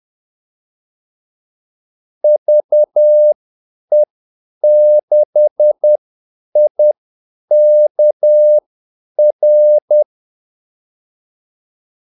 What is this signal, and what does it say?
Morse code, heard as a keyed tone, about 10 wpm, spaced as for 8 wpm, 600 Hz. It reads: VE6IKR
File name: Callsigns-10-8wpm